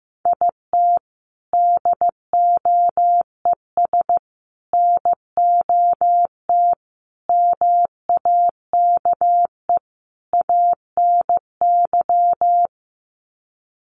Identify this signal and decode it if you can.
Morse code, heard as a keyed tone, about 15 wpm, 700 Hz. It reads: IT DOES NOT MAKE ANY
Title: Sets of 5 Words Encore2 15wpm